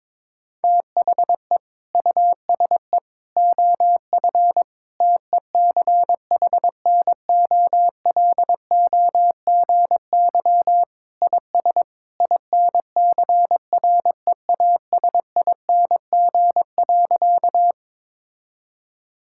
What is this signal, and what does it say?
Morse code, heard as a keyed tone, about 23 wpm, 700 Hz. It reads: THE USE OF TECHNOLOGY IS INCREASING.
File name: practice / 22wpm / Sentences with 2-Word Phrases - 22wpm